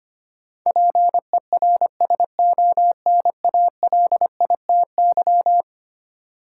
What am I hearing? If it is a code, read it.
Morse code, heard as a keyed tone, about 25 wpm, 700 Hz. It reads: PERSONALITY
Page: morse-code-ninja-practice-25wpm